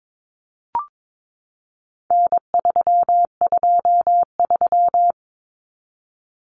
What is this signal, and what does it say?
Morse code, heard as a keyed tone, about 22 wpm, 700 Hz. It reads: N323
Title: Common Cut Numbers - 4-Digit 22wpm